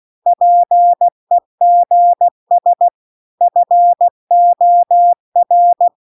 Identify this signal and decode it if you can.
Morse code, heard as a keyed tone, about 16 wpm, 700 Hz. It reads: PEGS FOR